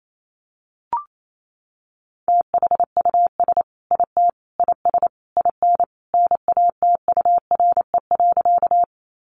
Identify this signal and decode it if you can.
Morse code, heard as a keyed tone, about 28 wpm, 700 Hz. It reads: THUS IT IS IN NATURE.
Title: Sentences from Top 800 Words 28wpm